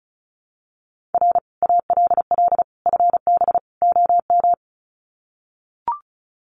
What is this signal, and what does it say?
Morse code, heard as a keyed tone, about 34 wpm, 700 Hz. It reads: R ALL FB OM
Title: Common QSO Phrases 35wpm